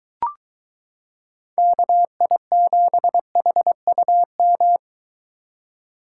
Morse code, heard as a keyed tone, about 23 wpm, 700 Hz. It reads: KI7HUM